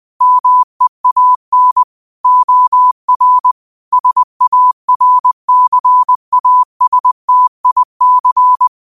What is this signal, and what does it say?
Morse code, heard as a keyed tone, about 20 wpm, 1000 Hz. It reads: MEAN OR SARCASTIC